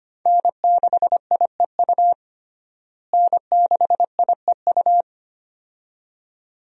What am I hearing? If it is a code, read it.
Morse code, heard as a keyed tone, about 25 wpm, 700 Hz. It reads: N6IEU N6IEU
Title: USA Call Signs - 5 Characters - Repeated 2x - 25wpm